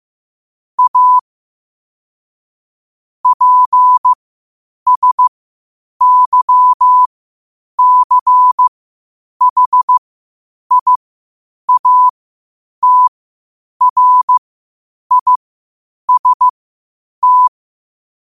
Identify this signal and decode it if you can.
Morse code, heard as a keyed tone, about 15 wpm, spaced as for 8 wpm, 1000 Hz. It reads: A PSYCHIATRIST
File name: qotdfiles / 8wpm